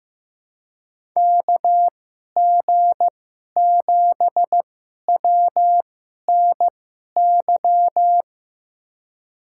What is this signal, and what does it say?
Morse code, heard as a keyed tone, about 15 wpm, spaced as for 11 wpm, 700 Hz. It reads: KG7WNY
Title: USA Call Signs - 6 Characters - 2x Character Spacing - 15wpm